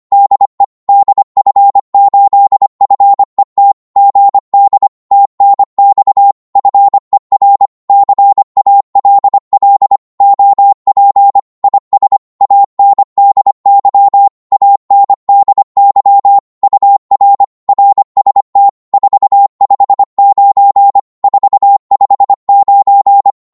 Morse code, heard as a keyed tone, about 25 wpm, 800 Hz. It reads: DE DF8FET GD TNX FER CALL OP IS ANDY ANDY UR RST 459 459
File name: Morse / qso-25wpm